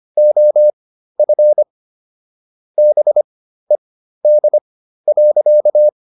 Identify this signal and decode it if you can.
Morse code, heard as a keyed tone, about 25 wpm, spaced as for 13 wpm, 600 Hz. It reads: OF BED.